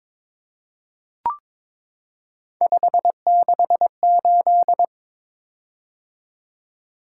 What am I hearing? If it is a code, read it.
Morse code, heard as a keyed tone, about 22 wpm, 700 Hz. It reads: 568
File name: practice / 22wpm / Three Letter-Number Combinations - 22wpm